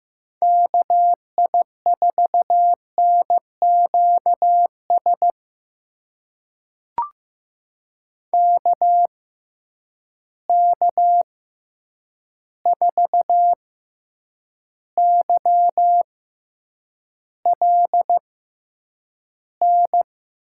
Morse code, heard as a keyed tone, about 15 wpm, spaced as for 8 wpm, 700 Hz. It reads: KI4NQS KK4YLN
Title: USA Call Signs - 6x Character Spacing - 15wpm